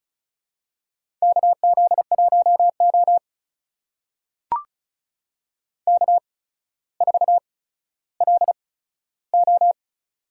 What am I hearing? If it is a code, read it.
Morse code, heard as a keyed tone, about 36 wpm, spaced as for 15 wpm, 700 Hz. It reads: KZ1O K4LO
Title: USA Call Signs - 4 Characters - 8x Character Spacing - 35wpm